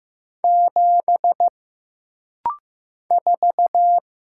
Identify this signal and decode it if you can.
Morse code, heard as a keyed tone, about 15 wpm, 700 Hz. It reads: 7 4
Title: Single Letter-Number Alt - Rapid-Fire 15wpm